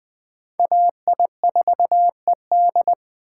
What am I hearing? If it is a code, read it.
Morse code, heard as a keyed tone, about 24 wpm, 700 Hz. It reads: AI4ED
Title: Speed Racing - USA 2x2 Call Signs 20wpm